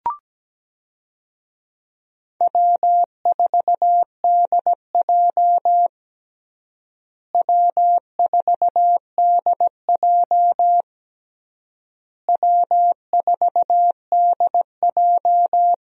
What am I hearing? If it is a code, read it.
Morse code, heard as a keyed tone, about 17 wpm, 700 Hz. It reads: W4DJ W4DJ W4DJ
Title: USA Call Signs - 4 Characters - Repeated 3x - 17wpm